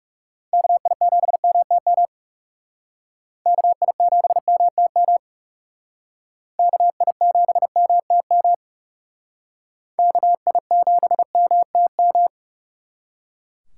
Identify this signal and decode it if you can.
Morse code, heard as a keyed tone, about 38 wpm, 700 Hz. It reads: KI7MTM KI7MTM KI7MTM KI7MTM